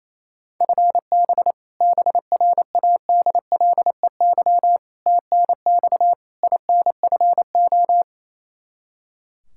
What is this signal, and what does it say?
Morse code, heard as a keyed tone, about 28 wpm, 700 Hz. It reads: FB BRADLEY TNX INFO
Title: Practice QSO - Exchange 3 - More Info - 28wpm